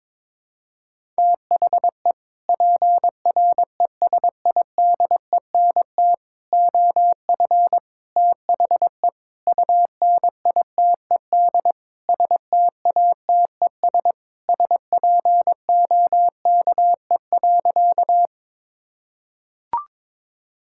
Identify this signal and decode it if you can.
Morse code, heard as a keyed tone, about 22 wpm, 700 Hz. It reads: THE PRESIDENT OF THE UNITED STATES SPOKE.
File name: practice / 22wpm / Sentences with 4-Word Phrases - 22wpm